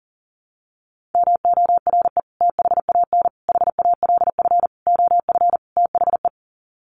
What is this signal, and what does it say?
Morse code, heard as a keyed tone, about 40 wpm, 700 Hz. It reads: MORE THAN HALF OF THE
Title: Sets of 5 Words 40wpm